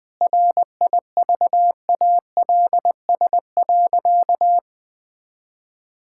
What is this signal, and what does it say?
Morse code, heard as a keyed tone, about 20 wpm, 700 Hz. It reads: RIVALS.
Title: Sentences with 7-Word Phrases - 20wpm